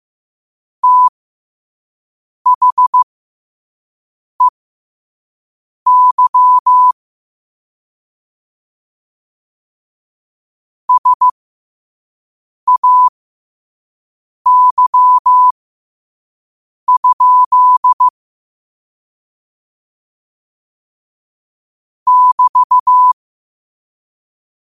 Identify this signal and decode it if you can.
Morse code, heard as a keyed tone, about 15 wpm, spaced as for 5 wpm, 1000 Hz. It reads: THEY SAY? <BT>
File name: qotdfiles / 5wpm